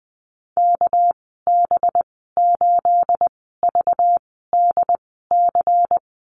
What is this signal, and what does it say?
Morse code, heard as a keyed tone, about 20 wpm, spaced as for 15 wpm, 700 Hz. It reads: KB8VDC
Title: 213 Lesson 34g - Number 8 - Call Signs 20wpm